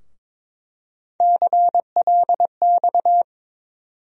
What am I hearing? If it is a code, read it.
Morse code, heard as a keyed tone, about 22 wpm, 700 Hz. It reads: CLX
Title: USA Call Sign Postfixes - 3 Letters 22wpm